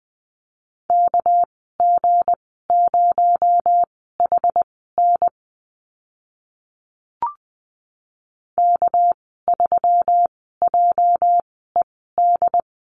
Morse code, heard as a keyed tone, about 20 wpm, spaced as for 14 wpm, 700 Hz. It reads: KG0HN K3JED
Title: 231 Lesson 37g - Letter x - Call Signs 20wpm